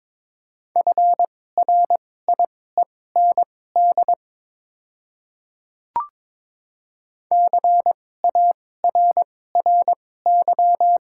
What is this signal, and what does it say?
Morse code, heard as a keyed tone, about 22 wpm, spaced as for 16 wpm, 690 Hz. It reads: FRIEND CARRY
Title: 2x Character Spacing - Top 500 Words - 22wpm